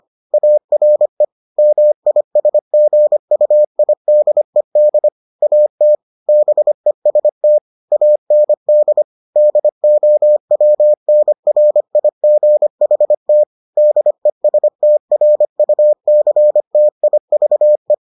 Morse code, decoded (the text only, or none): ARE MISGUIDED AT BEST AND DOWNRIGHT DESTRUCTIVE